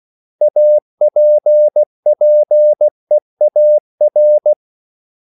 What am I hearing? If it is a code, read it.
Morse code, heard as a keyed tone, about 16 wpm, 600 Hz. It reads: APPEAR
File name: CW / Practice / 500MostCommonEnglishWords16